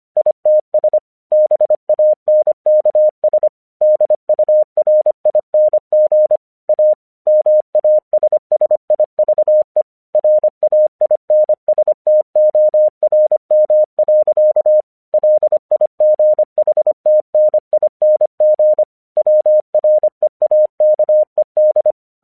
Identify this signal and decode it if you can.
Morse code, heard as a keyed tone, about 25 wpm, 610 Hz. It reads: ITS BANKS DURING A MASSIVE RAINSTORM. LIGHTNING WREAKED
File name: 1 English 25wpm / AQC06